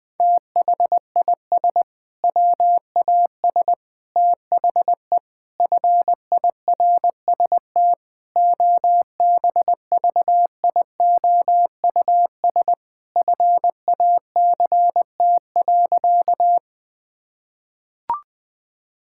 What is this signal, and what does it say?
Morse code, heard as a keyed tone, about 20 wpm, 700 Hz. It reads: THIS WAS THE FIRST OBVIOUS FACT.
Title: Sentences from Top 1600 Words 20wpm